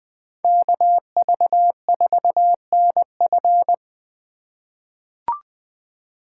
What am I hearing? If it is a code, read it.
Morse code, heard as a keyed tone, about 20 wpm, 700 Hz. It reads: KV4NF
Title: USA Call Signs - 5 Characters - 2x Character Spacing - 20wpm